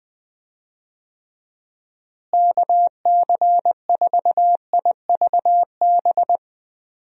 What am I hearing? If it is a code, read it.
Morse code, heard as a keyed tone, about 21 wpm, 700 Hz. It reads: KC4IVB
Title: USA Call Signs - 6 Characters - Repeated 3x - 20wpm